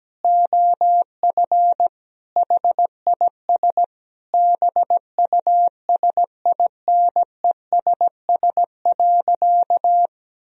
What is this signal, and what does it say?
Morse code, heard as a keyed tone, about 17 wpm, 700 Hz. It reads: OF HIS BUSINESS.
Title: Sentences from Top 1000 Words - Review Part 3 - 17wpm